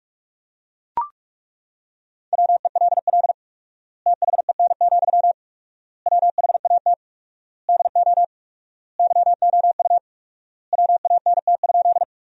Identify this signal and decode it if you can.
Morse code, heard as a keyed tone, about 42 wpm, 700 Hz. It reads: WELL THEN, WHAT DO YOU WANT?
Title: 4x Word-Spacing - Sentences from Top 200 Words - 45wpm